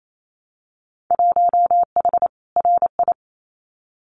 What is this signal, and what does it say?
Morse code, heard as a keyed tone, about 28 wpm, 700 Hz. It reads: 1H RI
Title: Winter Field Day Exchanges 28wpm